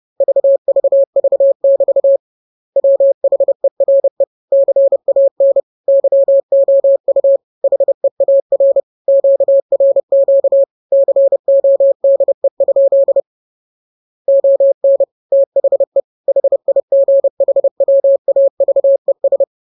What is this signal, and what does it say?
Morse code, heard as a keyed tone, about 30 wpm, 550 Hz. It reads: VVV<BT> WHERE CAN YOU HEAR QRQ CODE? ON THE HIGHWAVES